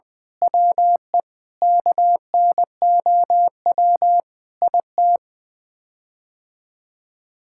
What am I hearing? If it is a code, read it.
Morse code, heard as a keyed tone, about 20 wpm, 700 Hz. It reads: WE KNOW IT
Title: Sets of 8 Words - 20wpm